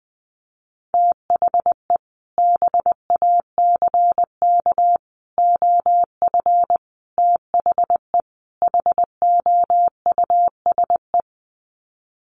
Morse code, heard as a keyed tone, about 20 wpm, 700 Hz. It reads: THE BACK OF THE HOUSE
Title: Sets of 5 Words Encore2 20wpm